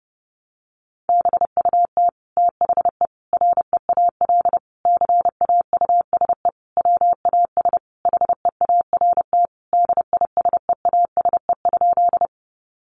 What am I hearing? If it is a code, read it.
Morse code, heard as a keyed tone, about 30 wpm, 700 Hz. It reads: BUT THE REAL CAUSE WAS HEART DISEASE?
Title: Sentences from Top 1200 Words 30wpm